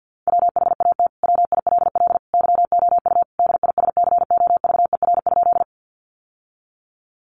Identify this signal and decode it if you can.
Morse code, heard as a keyed tone, about 50 wpm, 700 Hz. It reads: WHAT WILL YOU DISCOVER?
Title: Sentences from Top 1100 Words 50wpm